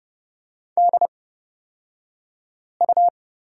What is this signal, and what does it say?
Morse code, heard as a keyed tone, about 30 wpm, spaced as for 5 wpm, 700 Hz. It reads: DU
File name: Lesson 38d - Letter q - Two Characters 30wpm